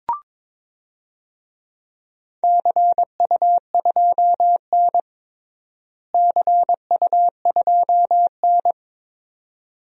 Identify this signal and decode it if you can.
Morse code, heard as a keyed tone, about 22 wpm, 700 Hz. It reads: CU2N CU2N
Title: DXCC-100 Call Signs - 4 Characters - Repeated 2x - 22wpm